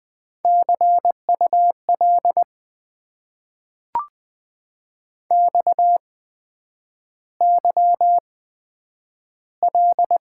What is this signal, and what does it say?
Morse code, heard as a keyed tone, about 20 wpm, spaced as for 9 wpm, 700 Hz. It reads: CUL XYL
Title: Top 90 QSO Elements - 8x Character Spacing - 20wpm